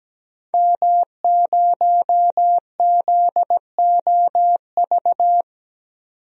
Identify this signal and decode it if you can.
Morse code, heard as a keyed tone, about 17 wpm, 700 Hz. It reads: M0ZOV
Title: DXCC-100 Call Signs - 5 Characters - 17wpm